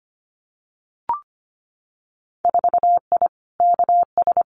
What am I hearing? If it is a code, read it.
Morse code, heard as a keyed tone, about 25 wpm, 700 Hz. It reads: 4I KS
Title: Winter Field Day Exchanges Encore 25wpm